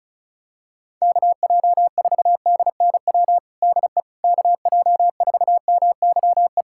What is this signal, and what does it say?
Morse code, heard as a keyed tone, about 34 wpm, 700 Hz. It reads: KJ4DNW DE KJ4MYE
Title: USA Callsign DE Callsign - 6 Characters - 35wpm